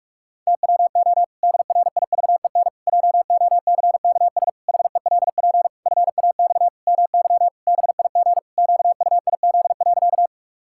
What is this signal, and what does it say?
Morse code, heard as a keyed tone, about 46 wpm, 700 Hz. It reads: TWO DRIVEN JOCKS HELP FAX MY BIG QUIZ.
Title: Pangram Challenge v2 - 45wpm